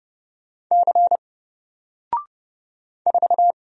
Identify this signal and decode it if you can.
Morse code, heard as a keyed tone, about 30 wpm, 700 Hz. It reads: C 4
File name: Single-Letter-Number-30wpm